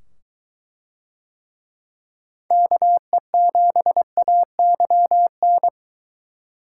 Morse code, heard as a keyed tone, about 22 wpm, 700 Hz. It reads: KE7AYN